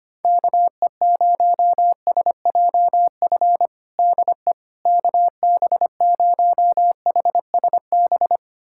Morse code, heard as a keyed tone, about 25 wpm, 700 Hz. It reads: KE0SJF DE KB0HSB
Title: USA Callsign DE Callsign - 6 Characters - 25wpm